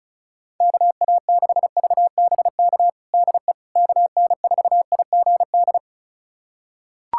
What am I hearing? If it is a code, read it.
Morse code, heard as a keyed tone, about 35 wpm, 700 Hz. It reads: KA6VBK DE KN4IGD
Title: USA Callsign DE Callsign 35wpm